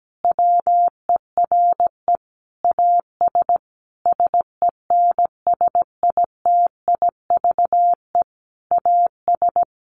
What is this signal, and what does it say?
Morse code, heard as a keyed tone, about 17 wpm, 700 Hz. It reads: WERE AS SENSITIVE AS
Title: Sentences from Top 1400 Words 17wpm